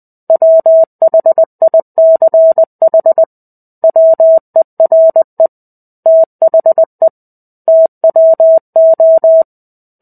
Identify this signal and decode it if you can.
Morse code, heard as a keyed tone, about 20 wpm, 650 Hz. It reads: WHICH WERE THE TWO
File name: 05-Chapter-05 20 WPM